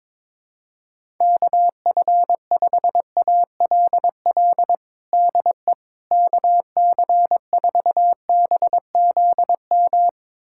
Morse code, heard as a keyed tone, about 22 wpm, 700 Hz. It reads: KF5ALL DE KC4BZM
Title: USA Callsign DE Callsign - 6 Characters - 22wpm